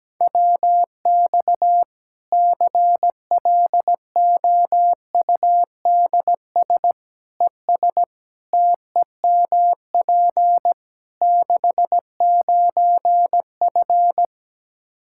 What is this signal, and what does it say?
Morse code, heard as a keyed tone, about 17 wpm, 700 Hz. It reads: WX CLOUDS ES TEMP 69F
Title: K1USN SST QSO Practice 17wpm